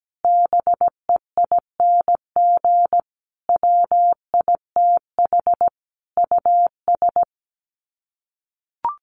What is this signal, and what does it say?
Morse code, heard as a keyed tone, about 17 wpm, 700 Hz. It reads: BEING WITH US
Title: Sets of 5 Words 17wpm